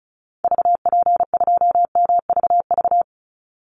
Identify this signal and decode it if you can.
Morse code, heard as a keyed tone, about 34 wpm, 700 Hz. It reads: VP2MVV
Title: Call Signs - 35wpm